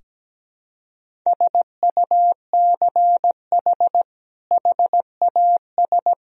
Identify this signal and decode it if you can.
Morse code, heard as a keyed tone, about 17 wpm, 700 Hz. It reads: SUCH HAS
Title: Sentences from Top 1000 Words - Review Part 3 - 17wpm